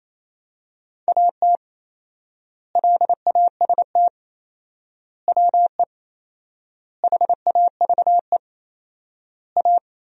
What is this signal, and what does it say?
Morse code, heard as a keyed tone, about 28 wpm, 700 Hz. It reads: AT LAST WE HAVE A
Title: 4x Word-Spacing - Sentences from Top 300 Words - 28wpm